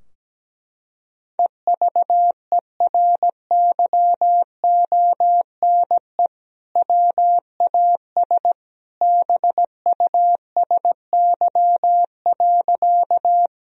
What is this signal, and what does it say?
Morse code, heard as a keyed tone, about 17 wpm, 700 Hz. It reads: EVERYONE WAS BUSY.
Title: Sentences from Top 1000 Words - Review Part 3 - 17wpm